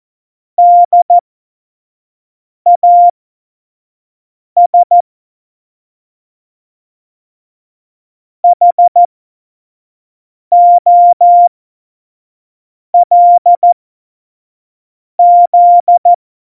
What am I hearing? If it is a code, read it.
Morse code, heard as a keyed tone, about 14 wpm, spaced as for 5 wpm, 700 Hz. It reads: DAS HOLZ
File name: Morsekurs / Witze / Witz 11. September 2018, 25 F70